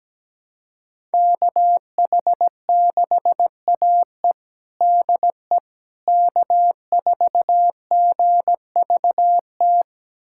Morse code, heard as a keyed tone, about 17 wpm, 700 Hz. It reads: KH6AE DE K4GVT
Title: USA Callsign DE Callsign - 5 Characters - 17wpm